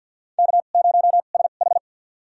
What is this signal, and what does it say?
Morse code, heard as a keyed tone, about 52 wpm, spaced as for 36 wpm, 700 Hz. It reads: K0SH
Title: USA Call Signs - 2x Character Spacing - 50wpm